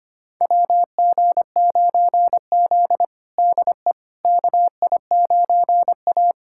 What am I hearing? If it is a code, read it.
Morse code, heard as a keyed tone, about 25 wpm, 700 Hz. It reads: WG9Z DE KI9A
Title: USA Callsign DE Callsign - 4 Characters - 25wpm